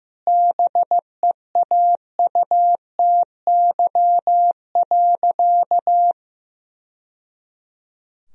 Morse code, heard as a keyed tone, about 14 wpm, 700 Hz. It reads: BEAUTY.
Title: Sentences from Top 1000 Words - Review Part 2 - 15wpm